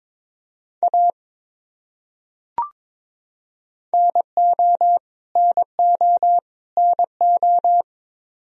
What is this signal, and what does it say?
Morse code, heard as a keyed tone, about 22 wpm, 700 Hz. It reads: A NO NO NO
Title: Top 100 Words Repeated 22wpm